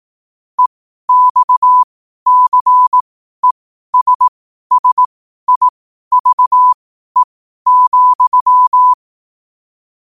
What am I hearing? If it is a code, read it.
Morse code, heard as a keyed tone, about 18 wpm, spaced as for 12 wpm, 1000 Hz. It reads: EXCESSIVE,
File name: qotdfiles / 12wpm